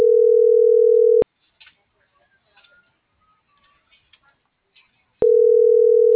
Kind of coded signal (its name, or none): telephone call progress tone